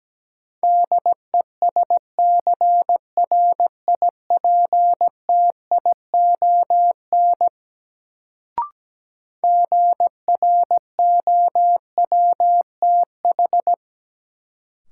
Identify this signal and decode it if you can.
Morse code, heard as a keyed tone, about 17 wpm, 700 Hz. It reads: DESCRIPTION GROWTH
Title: Sentences from Top 1300 Words - New Word Drill Alt - 17wpm